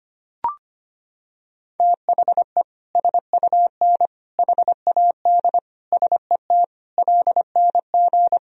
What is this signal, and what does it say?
Morse code, heard as a keyed tone, about 25 wpm, 700 Hz. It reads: THE SUN HAD SET LNG
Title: Sentences from Top 300 Words and Abbreviations 25wpm